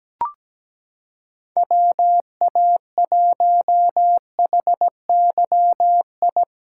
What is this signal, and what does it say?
Morse code, heard as a keyed tone, about 17 wpm, 700 Hz. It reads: WA1HYI
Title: USA Callsign DE Callsign - 6 Characters - 17wpm